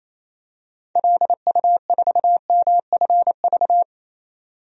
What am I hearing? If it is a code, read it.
Morse code, heard as a keyed tone, about 28 wpm, 700 Hz. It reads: LU4MFV
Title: DXCC-100 Call Signs - 6 Characters - 28wpm